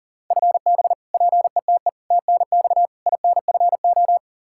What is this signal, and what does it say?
Morse code, heard as a keyed tone, about 40 wpm, 700 Hz. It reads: FB PETE TNX INFO
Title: Practice QSO - Exchange 3 - More Info - 40wpm